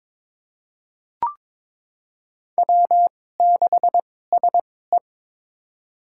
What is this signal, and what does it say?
Morse code, heard as a keyed tone, about 22 wpm, spaced as for 16 wpm, 700 Hz. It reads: W6SE
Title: USA Call Signs - 4 Characters - 2x Character Spacing - 22wpm